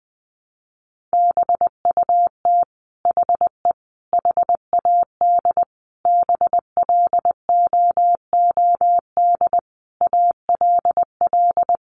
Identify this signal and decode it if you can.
Morse code, heard as a keyed tone, about 20 wpm, 700 Hz. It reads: BUT HE HAD BLOOD ALL
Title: Sentences from Top 1100 Words 20wpm